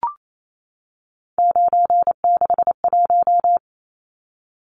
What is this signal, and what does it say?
Morse code, heard as a keyed tone, about 28 wpm, 700 Hz. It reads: 961
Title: Numbers - 3-Digit 28wpm